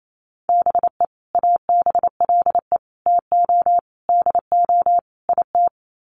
Morse code, heard as a keyed tone, about 28 wpm, 700 Hz. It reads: BE ABLE TO DO IT
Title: Sets of 5 Words Encore2 28wpm